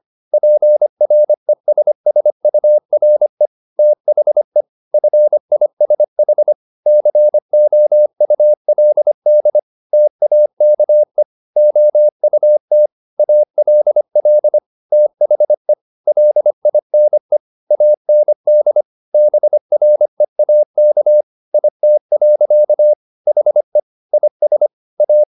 Morse code, heard as a keyed tone, about 25 wpm, 600 Hz. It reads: PRESSURE THE FISH COULD TAKE OUT ALL THE LINE AND BREAK IT. HE IS A
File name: the old man and the sea part 1 25wpm